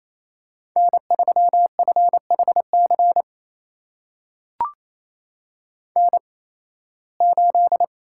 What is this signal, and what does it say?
Morse code, heard as a keyed tone, about 28 wpm, spaced as for 18 wpm, 700 Hz. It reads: N3FHC N8
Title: USA Call Signs - 8x Character Spacing - 28wpm